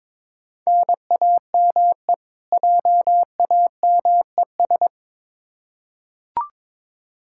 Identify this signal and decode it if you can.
Morse code, heard as a keyed tone, about 21 wpm, 700 Hz. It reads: NAME JAMES
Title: Practice QSO - Exchange 1 - Beginning - 22wpm